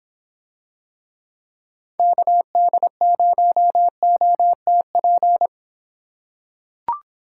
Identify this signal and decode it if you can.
Morse code, heard as a keyed tone, about 25 wpm, 700 Hz. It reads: KD0OTP